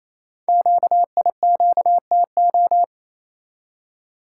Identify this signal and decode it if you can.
Morse code, heard as a keyed tone, about 29 wpm, 700 Hz. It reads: QIQTO
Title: Code Groups - 5 Letters - 28wpm